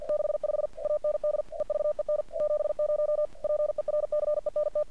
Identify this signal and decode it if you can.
Morse code, heard as a keyed tone, about 46 wpm, 600 Hz. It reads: 65 AND EVEN 70 PERCENT
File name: archived / CRSnet